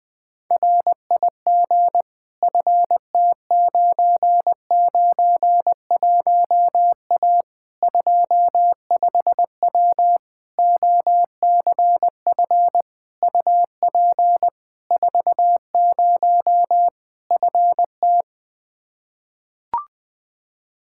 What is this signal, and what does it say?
Morse code, heard as a keyed tone, about 20 wpm, 700 Hz. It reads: RIG FT991A 25W OCF UP 40 FT
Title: Practice QSO - Exchange 2 - Info - 20wpm